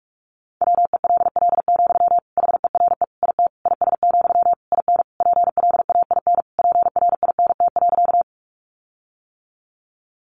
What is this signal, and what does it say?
Morse code, heard as a keyed tone, about 44 wpm, 700 Hz. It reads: WELL, HERE IT IS, IN PLAIN PRINT.
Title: Sentences from Top 1400 Words 45wpm